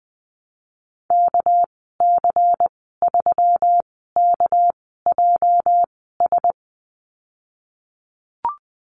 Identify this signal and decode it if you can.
Morse code, heard as a keyed tone, about 20 wpm, spaced as for 14 wpm, 700 Hz. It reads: KC3KJS
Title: 213 Lesson 34g - Number 8 - Call Signs 20wpm